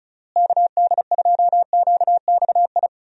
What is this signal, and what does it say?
Morse code, heard as a keyed tone, about 34 wpm, 700 Hz. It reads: KD2QXI
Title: USA Call Signs - 6 Characters - Repeated 3x - 35wpm